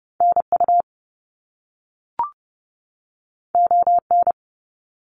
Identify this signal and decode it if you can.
Morse code, heard as a keyed tone, about 30 wpm, 700 Hz. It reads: NU ON
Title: Canadian Provinces and Territories 30wpm